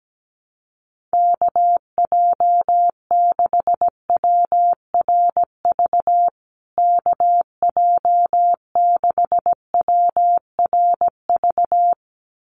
Morse code, heard as a keyed tone, about 17 wpm, 700 Hz. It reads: KJ6WRV KJ6WRV
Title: USA Call Signs Repeated 2x 17wpm